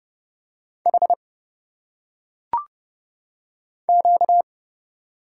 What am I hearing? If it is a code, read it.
Morse code, heard as a keyed tone, about 30 wpm, 700 Hz. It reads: H Q
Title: Single Letters - Phonetic Pronunciation 30wpm